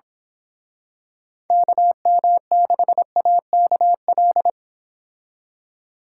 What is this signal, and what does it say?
Morse code, heard as a keyed tone, about 26 wpm, 700 Hz. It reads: KM6AKL
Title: Speed Racing - USA 2x3 Call Signs 22wpm